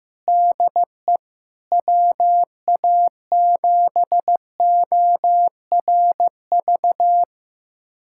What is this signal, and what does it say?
Morse code, heard as a keyed tone, about 15 wpm, 700 Hz. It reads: DE WA7ORV
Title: USA Callsign DE Callsign - 6 Characters - 15wpm